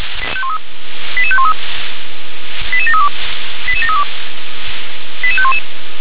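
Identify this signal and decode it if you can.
Five-tone selective call (ZVEI), heard as quick runs of five tones, tones of about 80 ms. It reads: 88513 88732 88732 88511
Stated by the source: ZVEI five-tone selective call received on SDR.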